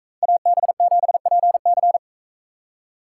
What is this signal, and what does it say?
Morse code, heard as a keyed tone, about 41 wpm, 700 Hz. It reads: AB7PC